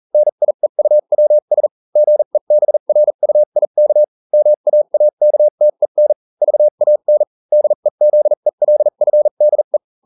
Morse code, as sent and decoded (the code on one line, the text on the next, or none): -. .. ..- .-- ... / --. -... .-. ..- .. -.- / -- .- .- -.- - . -. / ...- .- -. / -.. --.. .-.. ..-. -.. .
NIEUWS GEBRUIK MAAKTEN VAN DEZELFDE